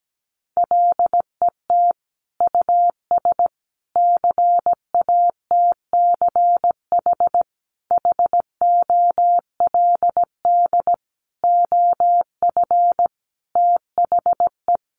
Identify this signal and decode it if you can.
Morse code, heard as a keyed tone, about 17 wpm, 700 Hz. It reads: LET US CATCH HOLD OF THE
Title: Sentences from Top 1500 Words 17wpm